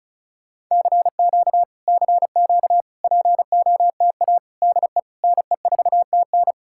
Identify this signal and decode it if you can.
Morse code, heard as a keyed tone, about 45 wpm, 700 Hz. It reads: CQ CQ POTA DE NE4TN